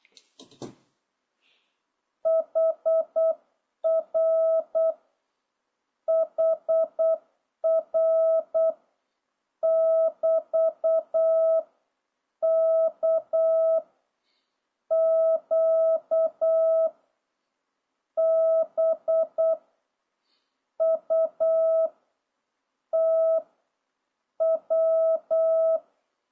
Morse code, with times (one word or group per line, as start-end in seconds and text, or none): 2.3-4.9 HR
6.1-8.7 HR
9.6-11.6 <BT>
12.4-13.8 K
14.9-16.9 Q
18.2-19.5 B
20.8-21.9 U
22.9-23.4 T
24.4-25.8 W